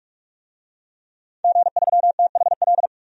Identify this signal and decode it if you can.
Morse code, heard as a keyed tone, about 46 wpm, 700 Hz. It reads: G3THL
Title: DXCC-100 Call Signs - 5 Characters - 45wpm